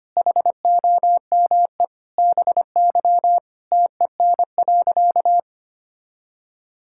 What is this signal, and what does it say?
Morse code, heard as a keyed tone, about 25 wpm, 700 Hz. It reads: HOME BY TEN.
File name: practice / 25wpm / Sentences with 4-Word Phrases - 25wpm